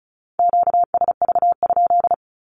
Morse code, heard as a keyed tone, about 36 wpm, 700 Hz. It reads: QSV?
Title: Q-Codes with Spoken Meaning - 35wpm